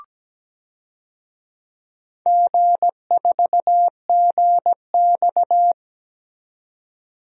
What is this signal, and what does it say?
Morse code, heard as a keyed tone, about 17 wpm, 700 Hz. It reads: G4GX
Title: DXCC-100 Call Signs - 4 Characters - 17wpm